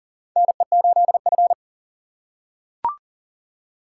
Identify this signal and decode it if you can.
Morse code, heard as a keyed tone, about 40 wpm, 700 Hz. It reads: NE8F
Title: USA Call Signs - 4 Characters - 8x Character Spacing - 40wpm